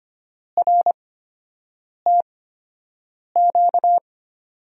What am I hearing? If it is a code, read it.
Morse code, heard as a keyed tone, about 25 wpm, spaced as for 7 wpm, 700 Hz. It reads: RTQ
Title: USA Call Signs - 6 Characters - 8x Character Spacing - 25wpm